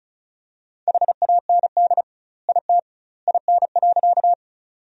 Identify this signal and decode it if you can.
Morse code, heard as a keyed tone, about 34 wpm, 700 Hz. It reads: HAND IT IN.
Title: 2x Word-Spacing - Sentences from Top 200 Words - 35wpm